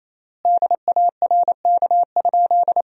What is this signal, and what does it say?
Morse code, heard as a keyed tone, about 28 wpm, 700 Hz. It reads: DARK?
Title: 3x Word-Spacing - Sentences from Top 500 Words - 28wpm